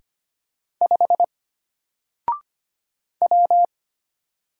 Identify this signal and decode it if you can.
Morse code, heard as a keyed tone, about 25 wpm, 700 Hz. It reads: 5 W
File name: Single-Letter-Number-25wpm